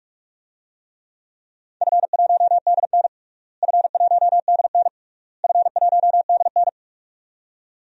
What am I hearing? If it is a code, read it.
Morse code, heard as a keyed tone, about 46 wpm, 700 Hz. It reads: F1DN F1DN F1DN
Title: DXCC-100 Call Signs - 4 Characters - Repeated 3x - 45wpm